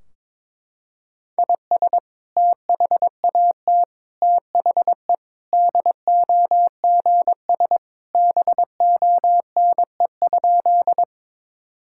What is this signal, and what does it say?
Morse code, heard as a keyed tone, about 22 wpm, 700 Hz. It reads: IS THAT THE DOGS BONE?
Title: Sentences from Top 1600 Words 22wpm